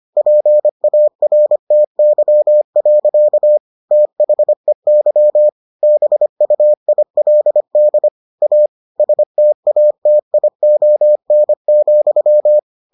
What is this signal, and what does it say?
Morse code, heard as a keyed tone, about 25 wpm, 600 Hz. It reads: PARTY. THEY BUILD A STATION,